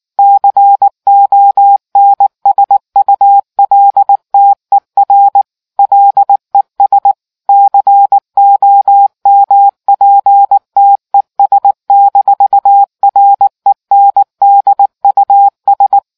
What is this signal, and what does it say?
Morse code, heard as a keyed tone, about 19 wpm, 800 Hz. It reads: CONSULTER LES COMPTES-RENDUS